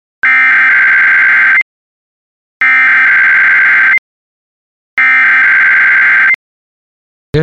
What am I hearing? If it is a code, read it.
Emergency Alert System (SAME) tones (header bursts), heard as harsh buzzing bursts. It is heard from 0.2 to 6.3 s.